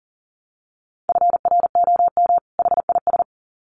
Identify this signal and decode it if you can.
Morse code, heard as a keyed tone, about 40 wpm, 700 Hz. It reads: FROM HIS
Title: Sets of 2 Words 40wpm